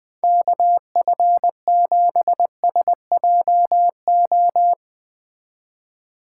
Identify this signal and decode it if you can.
Morse code, heard as a keyed tone, about 21 wpm, 700 Hz. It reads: KF7SJO